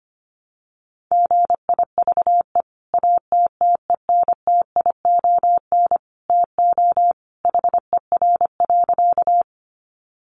Morse code, heard as a keyed tone, about 25 wpm, 700 Hz. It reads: GIVE ATTENTION TO HER.
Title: Sentences from Top 1000 Words 25wpm